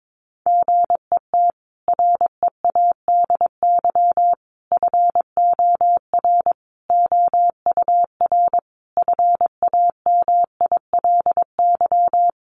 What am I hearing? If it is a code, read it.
Morse code, heard as a keyed tone, about 22 wpm, 700 Hz. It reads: GET READY FOR OUR FAMILY
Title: Sentences from Top 1200 Words 22wpm